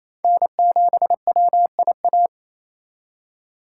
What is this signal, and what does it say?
Morse code, heard as a keyed tone, about 28 wpm, 700 Hz. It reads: N7WIA
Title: USA Call Signs - 5 Characters - 4x Character Spacing - 28wpm